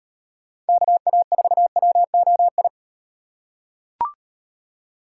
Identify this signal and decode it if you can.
Morse code, heard as a keyed tone, about 38 wpm, 700 Hz. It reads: KA4WOI